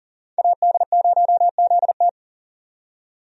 Morse code, heard as a keyed tone, about 40 wpm, 700 Hz. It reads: AD0ZT
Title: USA Call Signs - 5 Characters - 40wpm